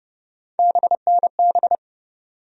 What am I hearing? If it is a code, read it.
Morse code, heard as a keyed tone, about 30 wpm, 700 Hz. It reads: BNB
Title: USA Call Sign Postfixes - 3 Letters 30wpm